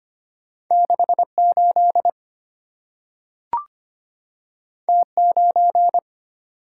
Morse code, heard as a keyed tone, about 25 wpm, 700 Hz. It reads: 68 T9
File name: Two Letter-Number Combinations - 25wpm